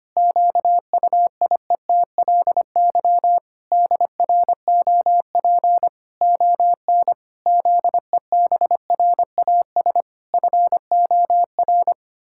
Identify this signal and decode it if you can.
Morse code, heard as a keyed tone, about 25 wpm, 700 Hz. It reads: QUIETLY DROP ON ZEBRAS FOR
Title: Pangram Challenge v2 - 25wpm